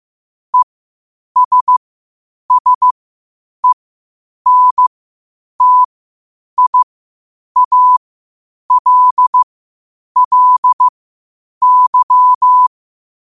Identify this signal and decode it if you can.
Morse code, heard as a keyed tone, about 15 wpm, spaced as for 8 wpm, 1000 Hz. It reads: ESSENTIALLY
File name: qotdfiles / 8wpm